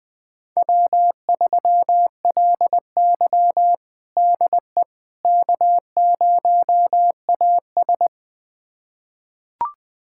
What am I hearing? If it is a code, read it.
Morse code, heard as a keyed tone, about 20 wpm, 700 Hz. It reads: W3LY DE K0AS
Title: USA Callsign DE Callsign - 4 Characters - 20wpm